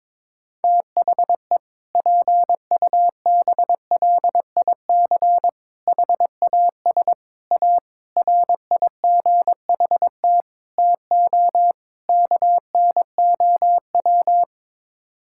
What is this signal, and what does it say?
Morse code, heard as a keyed tone, about 22 wpm, 700 Hz. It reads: THE PUBLIC HAS A RIGHT TO KNOW
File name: practice / 22wpm / Sets of 7 Words - 22wpm